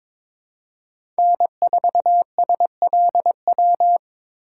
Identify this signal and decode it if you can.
Morse code, heard as a keyed tone, about 22 wpm, 700 Hz. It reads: N4SLW
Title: USA Call Signs - 5 Characters - 2x Character Spacing - 22wpm